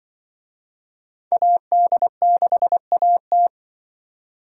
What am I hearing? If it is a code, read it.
Morse code, heard as a keyed tone, about 26 wpm, 700 Hz. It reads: AD6AT